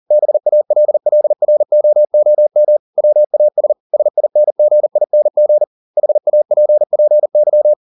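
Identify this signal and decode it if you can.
Morse code, heard as a keyed tone, about 40 wpm, 600 Hz. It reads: BALLROOM WAS SINGING HAPPY
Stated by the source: Morse code at 40 wpm, practice text, CW med Gnister.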